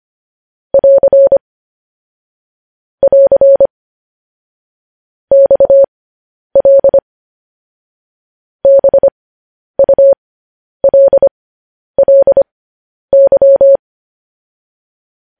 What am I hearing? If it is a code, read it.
Morse code, heard as a keyed tone, about 25 wpm, spaced as for 10 wpm, 550 Hz. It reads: <AR> <AR> XL BULLY
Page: News Headlines in Morse Code at 10 WPM